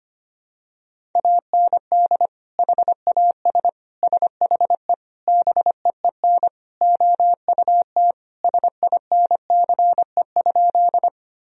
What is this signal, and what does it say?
Morse code, heard as a keyed tone, about 25 wpm, 700 Hz. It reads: AND HAS SHE BEEN OUT SINCE?
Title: Sentences from Top 1000 Words - Review Part 1 - 25wpm